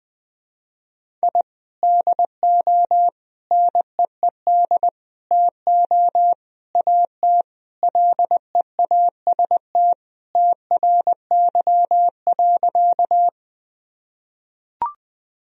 Morse code, heard as a keyed tone, about 20 wpm, 700 Hz. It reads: I DO NEED TO AT LEAST TRY.
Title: Sentences from Top 1000 Words - Review Part 1 - 20wpm